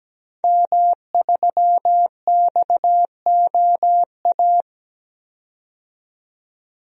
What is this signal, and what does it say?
Morse code, heard as a keyed tone, about 17 wpm, 700 Hz. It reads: M3XOA
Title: DXCC-100 Call Signs - 5 Characters - 17wpm